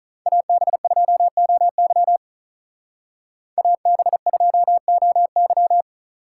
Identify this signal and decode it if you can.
Morse code, heard as a keyed tone, about 45 wpm, 700 Hz. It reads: AB2OY AB2OY